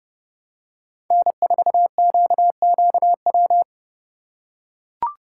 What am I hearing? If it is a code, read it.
Morse code, heard as a keyed tone, about 32 wpm, 700 Hz. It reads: N4QQW
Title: USA Call Signs - 5 Characters - 2x Character Spacing - 30wpm